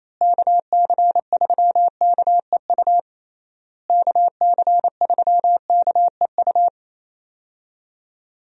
Morse code, heard as a keyed tone, about 28 wpm, 700 Hz. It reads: KC3KEU KC3KEU
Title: USA Call Signs - 6 Characters - Repeated 3x - 28wpm